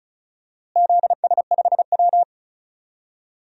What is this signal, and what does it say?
Morse code, heard as a keyed tone, about 34 wpm, 700 Hz. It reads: ZS5W
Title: DXCC-100 Call Signs - 4 Characters - Repeated 2x - 35wpm